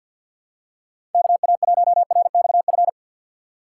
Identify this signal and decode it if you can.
Morse code, heard as a keyed tone, about 52 wpm, 700 Hz. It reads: KA1RXF
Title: USA Call Signs - 6 Characters - 4x Character Spacing - 50wpm